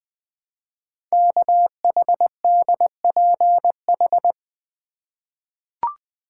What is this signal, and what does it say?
Morse code, heard as a keyed tone, about 20 wpm, 700 Hz. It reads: KHDPH
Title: Code Groups - 5 Letters - 20wpm